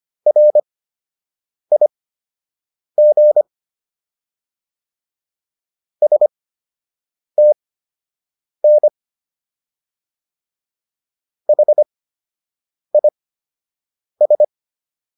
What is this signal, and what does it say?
Morse code, heard as a keyed tone, about 25 wpm, spaced as for 7 wpm, 600 Hz. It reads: RIG STN HIS